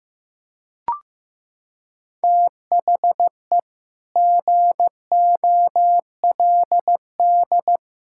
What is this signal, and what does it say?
Morse code, heard as a keyed tone, about 15 wpm, 700 Hz. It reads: THE GOLD
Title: Sentences from Top 1000 Words - Review Part 5 - 15wpm